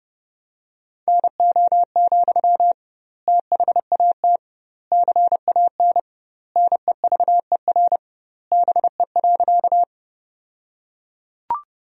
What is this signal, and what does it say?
Morse code, heard as a keyed tone, about 31 wpm, 700 Hz. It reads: NO, THAT CAN NEVER BE.
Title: 2x Word-Spacing - Sentences from Top 300 Words - 30wpm